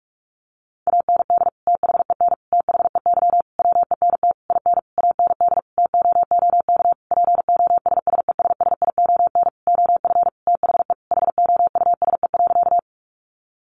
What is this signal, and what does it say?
Morse code, heard as a keyed tone, about 44 wpm, 700 Hz. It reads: AND THEN THEY WENT IN AND TOOK POSSESSION OF THE HOUSE.
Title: Sentences from Top 1400 Words 45wpm